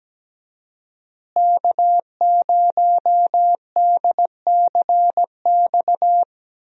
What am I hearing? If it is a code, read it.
Morse code, heard as a keyed tone, about 17 wpm, 700 Hz. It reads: K0DCX
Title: USA Call Signs - 5 Characters - Repeated 2x - 17wpm